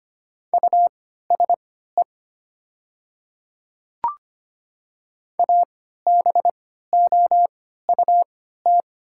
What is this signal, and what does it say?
Morse code, heard as a keyed tone, about 25 wpm, spaced as for 14 wpm, 700 Hz. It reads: USE ABOUT
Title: 3x Character Spacing - Top 100 Words - 25wpm